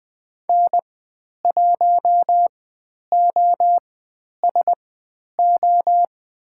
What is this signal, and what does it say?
Morse code, heard as a keyed tone, about 20 wpm, spaced as for 10 wpm, 700 Hz. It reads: N1OSO
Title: 102 Lesson 17g - Letter u - Call Signs 20wpm